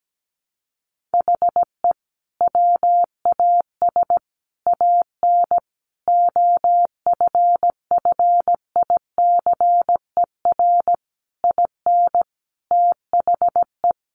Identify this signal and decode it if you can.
Morse code, heard as a keyed tone, about 17 wpm, 700 Hz. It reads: HE WAS AN OFFICER IN THE
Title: Sentences from Top 1200 Words 17wpm